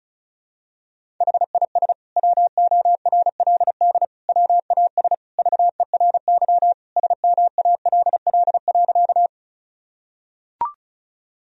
Morse code, heard as a keyed tone, about 36 wpm, 700 Hz. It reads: HIS WORLD WAS VERY SMALL.
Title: Sentences from Top 1000 Words - Review Part 1 - 35wpm